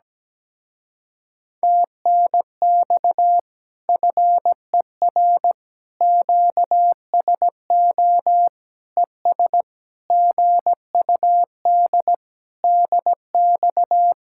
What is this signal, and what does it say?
Morse code, heard as a keyed tone, about 17 wpm, 700 Hz. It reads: TNX FER QSO ES GUD DX
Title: Speed-Racing Common QSO Phrases - 17wpm